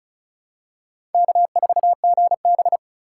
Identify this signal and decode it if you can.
Morse code, heard as a keyed tone, about 36 wpm, 700 Hz. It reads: K4GB
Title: USA Call Signs - 4 Characters - 3x Character Spacing - 35wpm